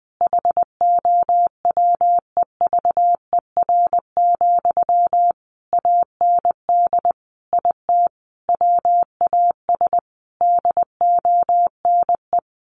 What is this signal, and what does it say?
Morse code, heard as a keyed tone, about 20 wpm, 700 Hz. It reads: HOWEVER, AND IT WAS DONE
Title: Sentences from Top 600 Words 20wpm